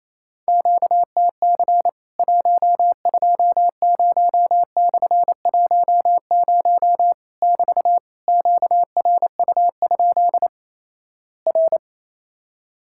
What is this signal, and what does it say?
Morse code, heard as a keyed tone, about 28 wpm, 700 Hz. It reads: QTC 120/10 <BT> QRU? R
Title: Worked All Europe DX Contest - QTC Exchange - 28wpm